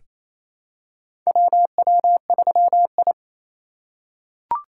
Morse code, heard as a keyed tone, about 28 wpm, 700 Hz. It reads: WW3I
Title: USA Call Signs - 4 Characters - 3x Character Spacing - 28wpm